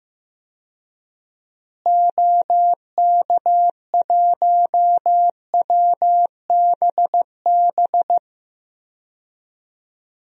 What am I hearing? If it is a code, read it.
Morse code, heard as a keyed tone, about 15 wpm, 700 Hz. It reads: OK1WBB
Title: DXCC-100 Call Signs - 6 Characters - Repeated 2x - 15wpm